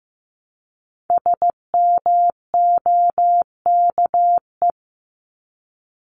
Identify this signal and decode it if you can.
Morse code, heard as a keyed tone, about 15 wpm, 700 Hz. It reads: SMOKE
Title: Top 1501-1750 Words 15wpm